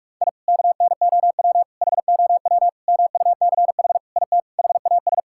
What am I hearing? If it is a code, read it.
Morse code, heard as a keyed tone, about 42 wpm, 700 Hz. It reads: I KNOW HOW MUCH IT HAS
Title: Sentences from Top 1000 Words - Review Part 5 - 45wpm